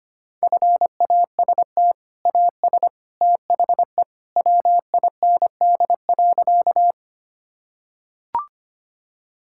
Morse code, heard as a keyed tone, about 25 wpm, 700 Hz. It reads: FAST AS THE WIND.
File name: practice / 25wpm / Sentences with 2-Word Phrases - 25wpm